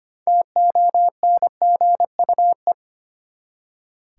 Morse code, heard as a keyed tone, about 25 wpm, 700 Hz. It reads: TONGUE
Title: Six Letter Words - 25wpm